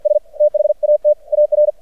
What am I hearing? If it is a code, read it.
Morse code, heard as a keyed tone, about 56 wpm, 600 Hz. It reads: S THAT AR
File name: archived / CRSnet